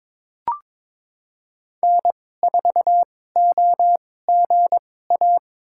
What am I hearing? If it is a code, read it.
Morse code, heard as a keyed tone, about 22 wpm, spaced as for 16 wpm, 700 Hz. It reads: N4OGA
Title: USA Call Signs - 5 Characters - 2x Character Spacing - 22wpm